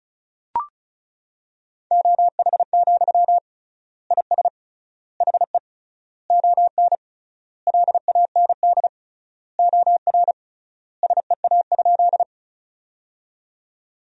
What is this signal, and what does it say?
Morse code, heard as a keyed tone, about 34 wpm, 700 Hz. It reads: OH, IS HE ON LAND OR SEA?
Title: 3x Word-Spacing - Sentences from Top 500 Words - 35wpm